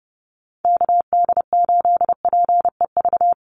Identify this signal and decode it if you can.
Morse code, heard as a keyed tone, about 30 wpm, 700 Hz. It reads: KD8PEV
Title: USA Call Signs 30wpm